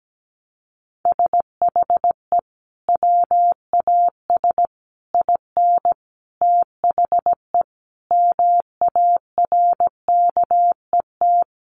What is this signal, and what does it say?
Morse code, heard as a keyed tone, about 17 wpm, 700 Hz. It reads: SHE WAS IN THE MARKET
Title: Sentences from Top 600 Words 17wpm